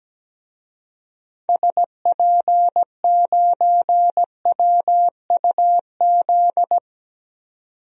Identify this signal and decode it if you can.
Morse code, heard as a keyed tone, about 17 wpm, 700 Hz. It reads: SP9WUZ
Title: DXCC-100 Call Signs - 6 Characters - 17wpm